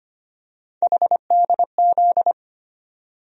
Morse code, heard as a keyed tone, about 25 wpm, 700 Hz. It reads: HDZ
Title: USA Call Sign Postfixes - 3 Letters 25wpm